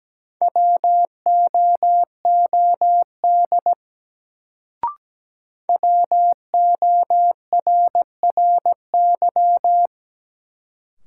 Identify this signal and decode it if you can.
Morse code, heard as a keyed tone, about 17 wpm, 700 Hz. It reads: WOOD WORRY
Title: Sentences from Top 1100 Words - New Word Drill Alt - 17wpm